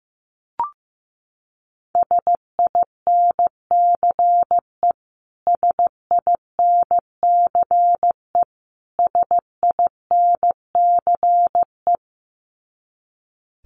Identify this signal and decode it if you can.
Morse code, heard as a keyed tone, about 15 wpm, 700 Hz. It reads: SINCE SINCE SINCE
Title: Top 500 Words Repeated 15wpm